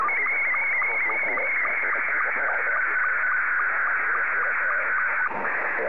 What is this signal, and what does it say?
SSTV (slow-scan television), Scottie 2, with speech.